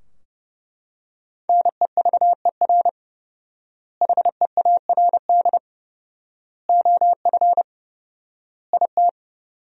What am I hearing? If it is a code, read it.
Morse code, heard as a keyed tone, about 30 wpm, 700 Hz. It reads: NEVER HEARD OF IT
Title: 4x Word-Spacing - Sentences from Top 400 Words - 30wpm